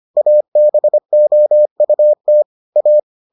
Morse code, heard as a keyed tone, about 25 wpm, 600 Hz. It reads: ABOUT A